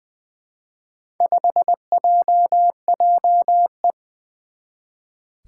Morse code, heard as a keyed tone, about 20 wpm, 700 Hz. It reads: 5JJE
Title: Four Challenging Letter-Number Combinations 20wpm